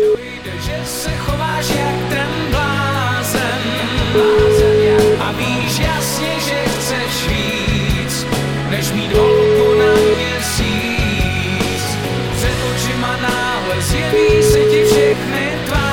Ringback tone (Europe, 425 Hz), with noise.